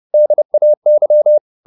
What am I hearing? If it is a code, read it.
Morse code, heard as a keyed tone, about 30 wpm, 600 Hz. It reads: DAY